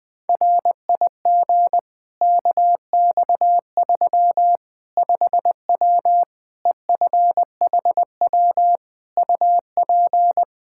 Morse code, heard as a keyed tone, about 20 wpm, 700 Hz. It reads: RIG KX3 5W EFHW UP
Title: Practice QSO - Exchange 2 - Info - 20wpm